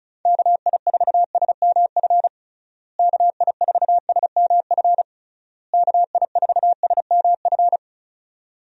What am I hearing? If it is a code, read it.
Morse code, heard as a keyed tone, about 34 wpm, 700 Hz. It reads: KI4SMF KI4SMF KI4SMF
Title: USA Call Signs - 6 Characters - Repeated 3x - 35wpm